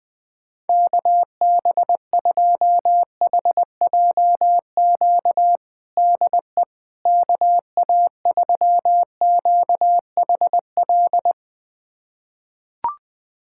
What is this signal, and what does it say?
Morse code, heard as a keyed tone, about 20 wpm, 700 Hz. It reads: KB2HJQ DE KA3QHL
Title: USA Callsign DE Callsign 20wpm